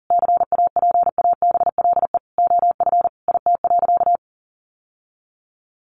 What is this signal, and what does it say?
Morse code, heard as a keyed tone, about 40 wpm, 700 Hz. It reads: CAPABLE OF IT.
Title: Sentences from Top 1200 Words 40wpm